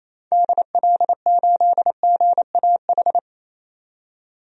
Morse code, heard as a keyed tone, about 27 wpm, 700 Hz. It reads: DL8GAH
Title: DXCC-100 Call Signs - 6 Characters - 28wpm